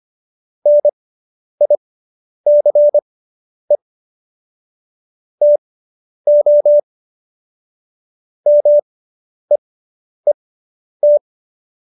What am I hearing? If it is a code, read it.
Morse code, heard as a keyed tone, about 25 wpm, spaced as for 10 wpm, 590 Hz. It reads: NICE TO MEET
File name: QSO220 10